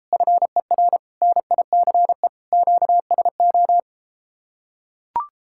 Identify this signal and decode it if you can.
Morse code, heard as a keyed tone, about 33 wpm, 700 Hz. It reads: FER NICE QSO